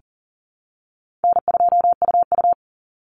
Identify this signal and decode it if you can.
Morse code, heard as a keyed tone, about 40 wpm, 700 Hz. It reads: N2UU
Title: Call Signs Repeated 3x - 40wpm